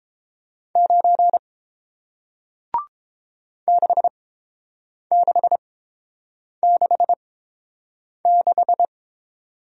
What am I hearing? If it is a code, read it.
Morse code, heard as a keyed tone, about 30 wpm, spaced as for 7 wpm, 700 Hz. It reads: 9 6666